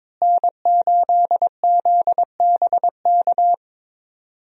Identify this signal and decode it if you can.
Morse code, heard as a keyed tone, about 22 wpm, 700 Hz. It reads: N8ZBK
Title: USA Call Signs - 5 Characters - Repeated 3x - 22wpm